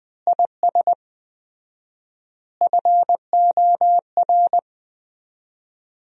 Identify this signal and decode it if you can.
Morse code, heard as a keyed tone, about 20 wpm, 700 Hz. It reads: IS FOR
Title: 4x Word-Spacing - Sentences from Top 400 Words - 20wpm